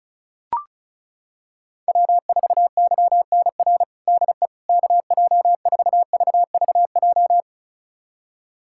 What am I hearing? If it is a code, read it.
Morse code, heard as a keyed tone, about 36 wpm, 700 Hz. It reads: W4YNR DE KJ4VVJ
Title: USA Callsign DE Callsign 35wpm